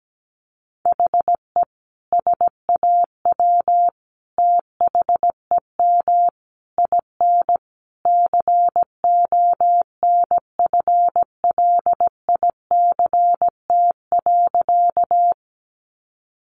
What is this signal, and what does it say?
Morse code, heard as a keyed tone, about 17 wpm, 700 Hz. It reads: HE SAW THEM IN CONFLICT.
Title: Sentences from Top 1300 Words 17wpm